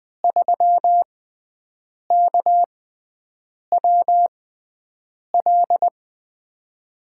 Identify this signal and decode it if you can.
Morse code, heard as a keyed tone, about 20 wpm, spaced as for 7 wpm, 700 Hz. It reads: 3KWL
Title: USA Call Signs - 6 Characters - 6x Character Spacing - 20wpm